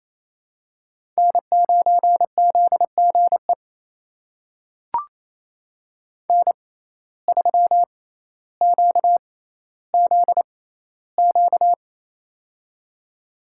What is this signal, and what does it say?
Morse code, heard as a keyed tone, about 28 wpm, spaced as for 14 wpm, 700 Hz. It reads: N9ZGE N3QZQ
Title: USA Call Signs - 5 Characters - 6x Character Spacing - 28wpm